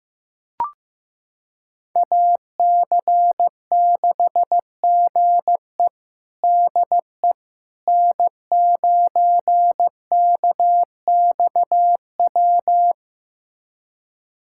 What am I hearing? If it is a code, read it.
Morse code, heard as a keyed tone, about 15 wpm, 700 Hz. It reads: AC6GE DE N9KXW
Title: USA Callsign DE Callsign - 5 Characters - 15wpm